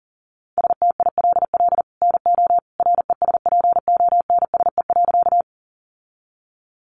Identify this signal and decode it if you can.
Morse code, heard as a keyed tone, about 40 wpm, 700 Hz. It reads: STILL NO RESPONSE.
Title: Sentences from Top 1400 Words 40wpm